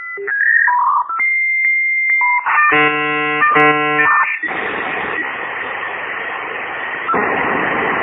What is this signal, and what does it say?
Dial-up modem handshake.